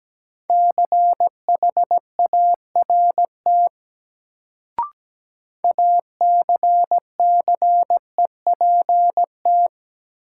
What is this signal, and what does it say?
Morse code, heard as a keyed tone, about 17 wpm, 700 Hz. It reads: CHART ACCEPT
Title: Sentences from Top 1300 Words - New Word Drill - 17wpm